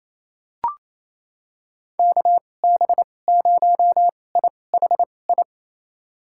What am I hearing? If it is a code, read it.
Morse code, heard as a keyed tone, about 28 wpm, spaced as for 20 wpm, 700 Hz. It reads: KB0IHI
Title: USA Call Signs - 2x Character Spacing - 28wpm